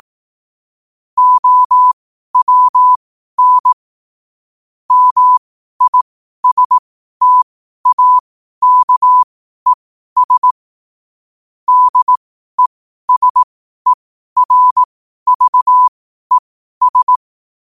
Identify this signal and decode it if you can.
Morse code, heard as a keyed tone, about 18 wpm, spaced as for 12 wpm, 1000 Hz. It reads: OWN MISTAKES DESERVES
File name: qotdfiles / 12wpm